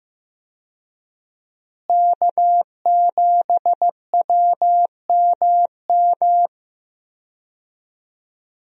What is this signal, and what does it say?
Morse code, heard as a keyed tone, about 15 wpm, 700 Hz. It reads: K7WMM
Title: USA Call Signs - 5 Characters - Repeated 3x - 15wpm